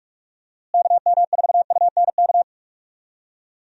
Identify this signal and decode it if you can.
Morse code, heard as a keyed tone, about 46 wpm, 700 Hz. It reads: KM4UNK
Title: USA Call Signs - 6 Characters - Repeated 2x - 45wpm